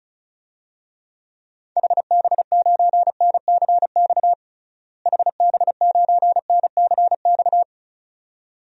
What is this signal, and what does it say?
Morse code, heard as a keyed tone, about 35 wpm, 700 Hz. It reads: HB9NCX HB9NCX
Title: DXCC-100 Call Signs - 6 Characters - Repeated 2x - 35wpm